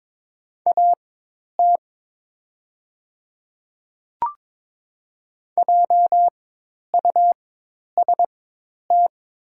Morse code, heard as a keyed tone, about 22 wpm, spaced as for 10 wpm, 700 Hz. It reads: AT JUST
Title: 4x Character Spacing - Top 200 Words - 22wpm